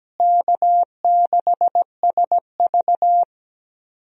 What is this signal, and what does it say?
Morse code, heard as a keyed tone, about 17 wpm, 700 Hz. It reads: K6SV
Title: USA Call Signs - 4 Characters - Repeated 2x - 17wpm